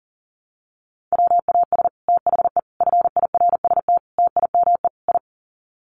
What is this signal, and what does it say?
Morse code, heard as a keyed tone, about 37 wpm, 700 Hz. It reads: WAS THE FIRST TIME I
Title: Sets of 5 Words Encore 40wpm